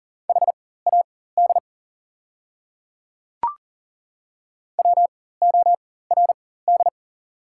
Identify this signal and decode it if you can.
Morse code, heard as a keyed tone, about 40 wpm, spaced as for 19 wpm, 700 Hz. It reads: HAD WORD
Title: 4x Character Spacing - Top 100 Words - 40wpm